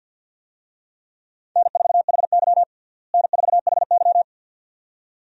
Morse code, heard as a keyed tone, about 52 wpm, 700 Hz. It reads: N4HY N4HY
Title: USA Call Signs - 4 Characters - Repeated 2x - 50wpm